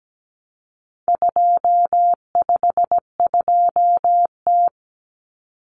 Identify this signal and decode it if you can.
Morse code, heard as a keyed tone, about 17 wpm, 700 Hz. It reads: 252T
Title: Common Cut Numbers - 4-Digit 17wpm